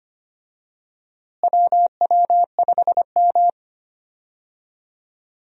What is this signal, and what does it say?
Morse code, heard as a keyed tone, about 25 wpm, 700 Hz. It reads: WW5M
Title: USA Call Signs - 4 Characters - Repeated 2x - 25wpm